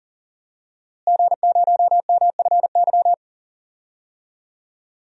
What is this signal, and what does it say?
Morse code, heard as a keyed tone, about 40 wpm, 700 Hz. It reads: G0MFY
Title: DXCC-100 Call Signs - 5 Characters - Repeated 3x - 40wpm